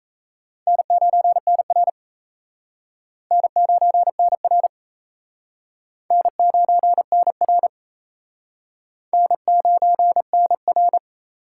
Morse code, heard as a keyed tone, about 35 wpm, 700 Hz. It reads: N9NR N9NR N9NR N9NR
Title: Speed Racing - USA 1x2 Call Signs 28wpm